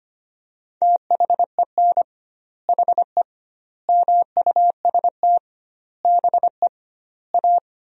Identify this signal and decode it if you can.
Morse code, heard as a keyed tone, about 25 wpm, 700 Hz. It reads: THEN HE MUST BE A
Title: 2x Word-Spacing - Sentences from Top 500 Words - 25wpm